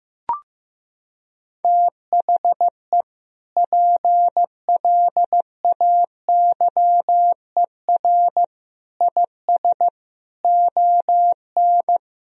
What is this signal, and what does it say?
Morse code, heard as a keyed tone, about 15 wpm, 700 Hz. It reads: THE PLAYER IS ON
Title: Sentences from Top 1000 Words - Review Part 4 - 15wpm